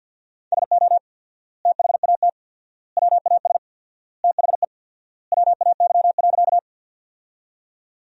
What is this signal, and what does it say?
Morse code, heard as a keyed tone, about 50 wpm, 700 Hz. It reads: SO THAT WAS THE WAY.
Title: 4x Word-Spacing - Sentences from Top 100 Words - 50wpm